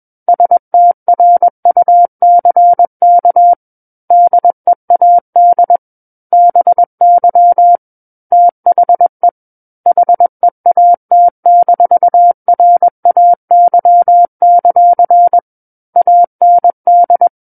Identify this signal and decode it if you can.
Morse code, heard as a keyed tone, about 21 wpm, 700 Hz. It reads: STRUCK DEAD BY THE HEAT-RAY; AND
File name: Book One / 12-Chapter 12 - 21 WPM